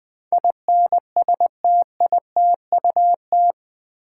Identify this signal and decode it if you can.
Morse code, heard as a keyed tone, about 20 wpm, 700 Hz. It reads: INSTITUT
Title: Build A Word - 4 Letters At A Time - 11-Letter Words - 20wpm